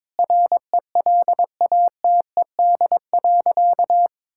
Morse code, heard as a keyed tone, about 22 wpm, 700 Hz. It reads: RELATED.
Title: Sentences from Top 1000 Words - Review Part 5 - 22wpm